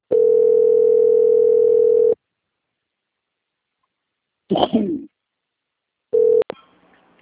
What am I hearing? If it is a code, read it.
Ringback tone (North America), 440 + 480 Hz.